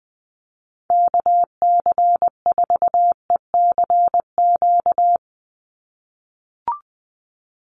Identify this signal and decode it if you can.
Morse code, heard as a keyed tone, about 20 wpm, 700 Hz. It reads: KC4ECQ
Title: USA Call Signs 20wpm